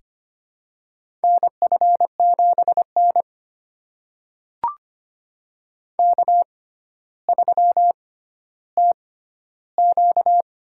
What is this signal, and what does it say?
Morse code, heard as a keyed tone, about 25 wpm, spaced as for 13 wpm, 700 Hz. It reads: NF7N K3TQ
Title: USA Call Signs - 4 Characters - 6x Character Spacing - 25wpm